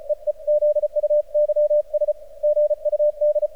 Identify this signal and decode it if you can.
Morse code, heard as a keyed tone, about 36 wpm, 600 Hz. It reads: EE ZUYS GUD
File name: archived / CRSnet